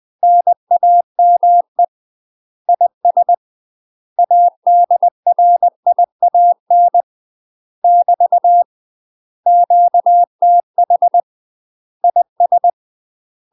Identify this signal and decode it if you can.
Morse code, heard as a keyed tone, about 20 wpm, 700 Hz. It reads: NAME IS ADRIAN <BT> QTH IS
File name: UK MASTER QSOs TRAINING / 20WPM / UK MASTER QSOs 0040